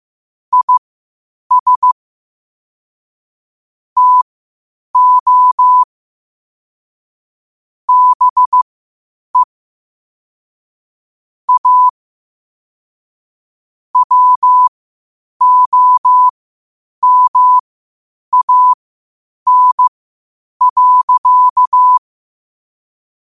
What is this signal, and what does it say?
Morse code, heard as a keyed tone, about 15 wpm, spaced as for 8 wpm, 1000 Hz. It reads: IS TO BE A WOMAN.
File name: qotdfiles / 8wpm